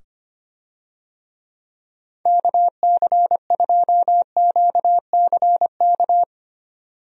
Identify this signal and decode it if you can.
Morse code, heard as a keyed tone, about 25 wpm, 700 Hz. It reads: KC2QCK
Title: USA Call Signs - 6 Characters - 25wpm